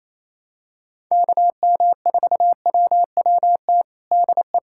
Morse code, heard as a keyed tone, about 27 wpm, 700 Hz. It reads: KM4WWT DE
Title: USA Callsign DE Callsign 28wpm